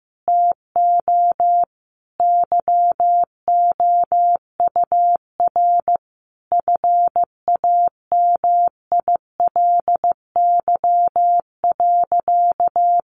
Morse code, heard as a keyed tone, about 15 wpm, 700 Hz. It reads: TO YOUR FAMILY.
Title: Sentences from Top 1400 Words 15wpm